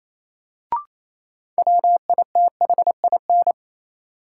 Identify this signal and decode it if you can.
Morse code, heard as a keyed tone, about 28 wpm, 700 Hz. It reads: WITHIN
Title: Sets of 2 Words - New Word Drill - 28wpm